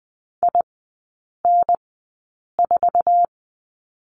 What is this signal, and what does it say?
Morse code, heard as a keyed tone, about 20 wpm, spaced as for 8 wpm, 700 Hz. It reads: IN4
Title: 074 Lesson 13e - Letter L - Three Characters 20wpm